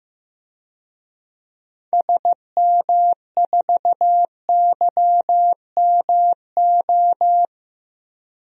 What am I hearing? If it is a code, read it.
Morse code, heard as a keyed tone, about 16 wpm, 700 Hz. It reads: SM4YMO
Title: DXCC-100 Call Signs - 6 Characters - 15wpm